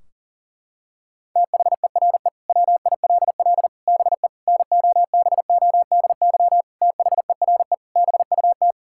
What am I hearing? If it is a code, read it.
Morse code, heard as a keyed tone, about 40 wpm, 700 Hz. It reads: THERE WILL BE NOBODY THERE BUT
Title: Sentences from Top 1600 Words 40wpm